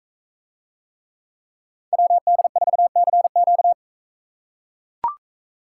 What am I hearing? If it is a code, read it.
Morse code, heard as a keyed tone, about 41 wpm, 700 Hz. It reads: WD4CQ